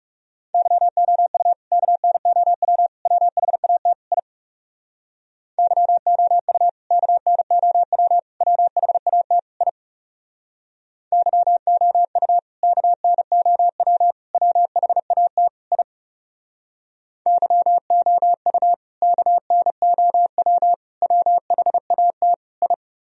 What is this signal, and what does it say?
Morse code, heard as a keyed tone, about 39 wpm, 700 Hz. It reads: YOU KNOW WHAT I YOU KNOW WHAT I YOU KNOW WHAT I YOU KNOW WHAT I